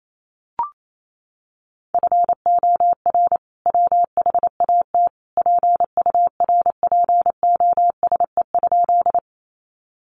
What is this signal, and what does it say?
Morse code, heard as a keyed tone, about 28 wpm, 700 Hz. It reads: FOR WHAT PURPOSE?
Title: Sentences from Top 800 Words 28wpm